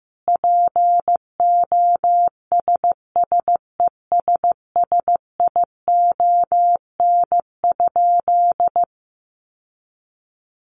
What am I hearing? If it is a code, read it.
Morse code, heard as a keyed tone, about 14 wpm, 700 Hz. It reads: POSSESSION?
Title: Sentences from Top 1400 Words 15wpm